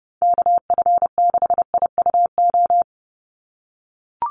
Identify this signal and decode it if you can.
Morse code, heard as a keyed tone, about 30 wpm, 700 Hz. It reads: KF6IUO
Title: USA Call Signs 30wpm